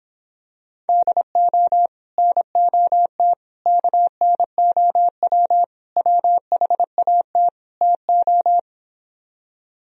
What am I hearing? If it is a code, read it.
Morse code, heard as a keyed tone, about 26 wpm, 700 Hz. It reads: DO NOT KNOW WHAT TO